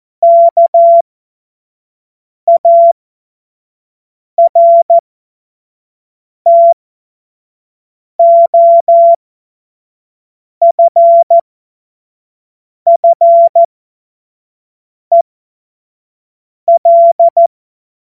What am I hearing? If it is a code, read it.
Morse code, heard as a keyed tone, about 14 wpm, spaced as for 5 wpm, 680 Hz. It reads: KARTOFFEL
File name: Morsekurs / Witze / Witz 06. April 2021, 25 F70